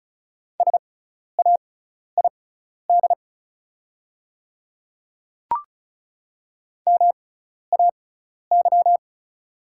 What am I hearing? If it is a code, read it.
Morse code, heard as a keyed tone, about 36 wpm, spaced as for 12 wpm, 700 Hz. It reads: SAID MAY
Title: 6x Character Spacing - Top 100 Words - 35wpm